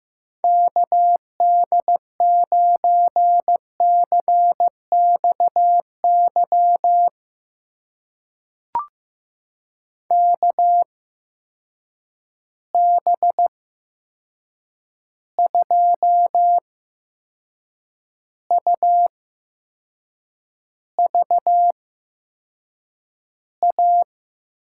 Morse code, heard as a keyed tone, about 15 wpm, spaced as for 6 wpm, 700 Hz. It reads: KD9CXY KB2UVA
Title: USA Call Signs - 6 Characters - 8x Character Spacing - 15wpm